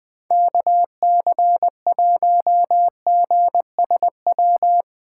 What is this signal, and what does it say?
Morse code, heard as a keyed tone, about 20 wpm, 700 Hz. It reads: KC1GSW
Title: Speed Racing - USA Call Signs 17wpm